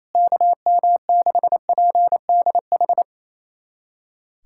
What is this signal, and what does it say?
Morse code, heard as a keyed tone, about 28 wpm, 700 Hz. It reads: KM6PDH
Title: USA Callsign DE Callsign - 6 Characters - 28wpm